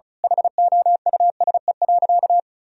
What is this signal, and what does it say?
Morse code, heard as a keyed tone, about 36 wpm, 700 Hz. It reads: HOUSE.
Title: Sentences from Top 1000 Words - Review Part 4 - 35wpm